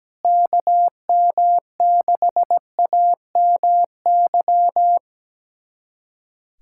Morse code, heard as a keyed tone, about 17 wpm, 700 Hz. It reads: KM6AMY
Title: USA Callsign DE Callsign - 6 Characters - 17wpm